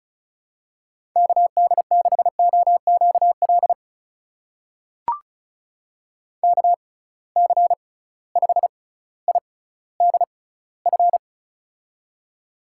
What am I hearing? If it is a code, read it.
Morse code, heard as a keyed tone, about 36 wpm, spaced as for 18 wpm, 700 Hz. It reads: KD6OQL KC5IDF
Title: USA Call Signs - 6 Characters - 6x Character Spacing - 35wpm